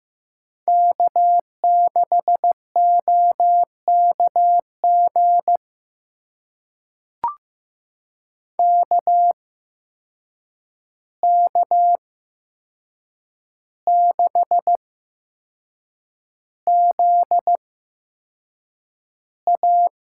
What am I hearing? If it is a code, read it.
Morse code, heard as a keyed tone, about 15 wpm, spaced as for 6 wpm, 700 Hz. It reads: K6OKG KK6ZA
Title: USA Call Signs - 8x Character Spacing - 15wpm